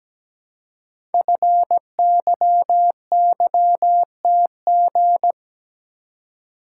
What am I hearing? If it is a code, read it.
Morse code, heard as a keyed tone, about 17 wpm, 700 Hz. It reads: FYYTG
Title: Code Groups - 5 Letters - 17wpm